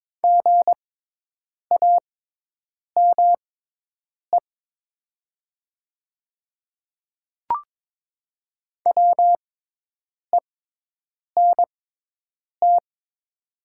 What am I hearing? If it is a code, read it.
Morse code, heard as a keyed tone, about 22 wpm, spaced as for 8 wpm, 700 Hz. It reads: GAME WENT
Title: 6x Character Spacing - Top 500 Words - 22wpm